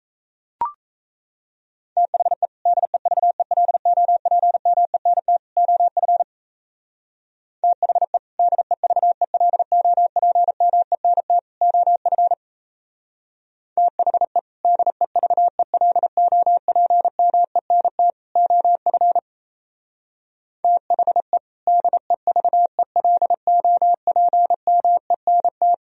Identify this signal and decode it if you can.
Morse code, heard as a keyed tone, about 33 wpm, 700 Hz. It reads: THE DEVELOPMENT OF THE DEVELOPMENT OF THE DEVELOPMENT OF THE DEVELOPMENT
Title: Speed-Racing - Sets of 3 Words 28wpm